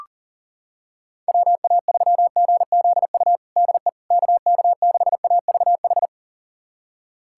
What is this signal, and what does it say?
Morse code, heard as a keyed tone, about 40 wpm, 700 Hz. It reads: WA3GZU DE KK6AVH
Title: USA Callsign DE Callsign - 6 Characters - 40wpm